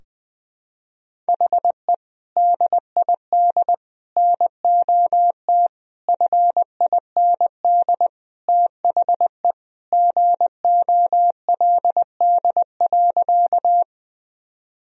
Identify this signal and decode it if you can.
Morse code, heard as a keyed tone, about 20 wpm, 700 Hz. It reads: HE DID NOT FIND THE GOLD.
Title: Sentences from Top 1000 Words - Review Part 3 - 20wpm